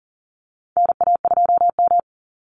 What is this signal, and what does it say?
Morse code, heard as a keyed tone, about 40 wpm, 700 Hz. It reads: NA2M
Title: Call Signs - 40wpm